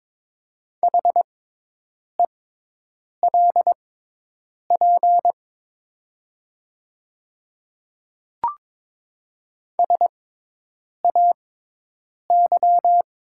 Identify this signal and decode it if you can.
Morse code, heard as a keyed tone, about 22 wpm, spaced as for 8 wpm, 700 Hz. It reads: HELP SAY
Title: 6x Character Spacing - Top 200 Words - 22wpm